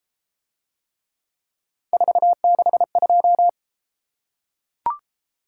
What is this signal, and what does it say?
Morse code, heard as a keyed tone, about 32 wpm, 700 Hz. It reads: 462